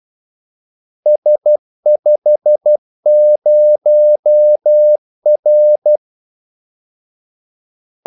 Morse code, heard as a keyed tone, about 12 wpm, 600 Hz. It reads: S50R
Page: 2 ekstra ordmellemrum mellem kaldesignalerne